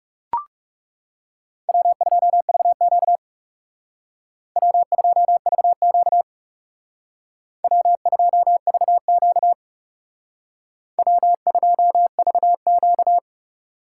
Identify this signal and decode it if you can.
Morse code, heard as a keyed tone, about 37 wpm, 700 Hz. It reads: W2VQ W2VQ W2VQ W2VQ
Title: Speed Racing - USA Call Signs 30wpm